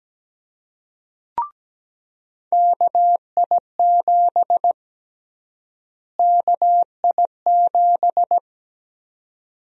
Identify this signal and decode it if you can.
Morse code, heard as a keyed tone, about 17 wpm, 700 Hz. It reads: KI7 KI7
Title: USA Call Sign Prefixes - Repeated 2x - 17wpm